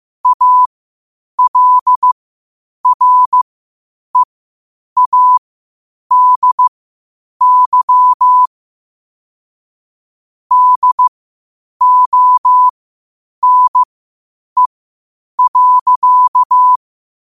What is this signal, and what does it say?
Morse code, heard as a keyed tone, about 15 wpm, spaced as for 8 wpm, 1000 Hz. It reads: ALREADY DONE.
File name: qotdfiles / 8wpm